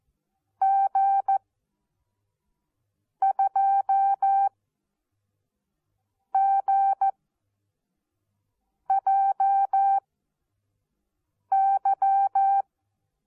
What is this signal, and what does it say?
Morse code, heard as a keyed tone, about 14 wpm, spaced as for 4 wpm, 780 Hz. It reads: G2GJY
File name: Onlinelehrgang / morsen / DJ4UF Morsen14